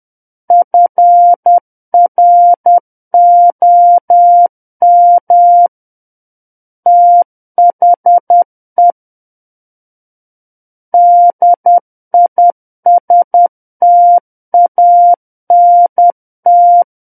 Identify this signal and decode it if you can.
Morse code, heard as a keyed tone, about 10 wpm, 700 Hz. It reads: FROM THE DISTANT
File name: Book One / 01-Chapter 1 - 10 WPM